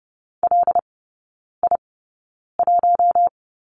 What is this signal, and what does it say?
Morse code, heard as a keyed tone, about 30 wpm, spaced as for 9 wpm, 690 Hz. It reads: LI1
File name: Lesson 17e - Letter u - Three Characters 30wpm